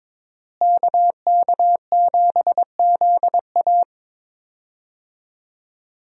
Morse code, heard as a keyed tone, about 22 wpm, 700 Hz. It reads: KK7ZA
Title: USA Call Signs - 5 Characters - Repeated 3x - 22wpm